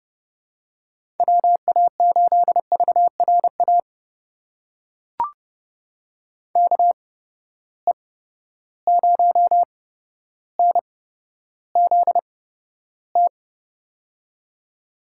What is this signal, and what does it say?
Morse code, heard as a keyed tone, about 30 wpm, spaced as for 13 wpm, 700 Hz. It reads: WA8VRA KE0NZT
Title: USA Call Signs - 6 Characters - 8x Character Spacing - 30wpm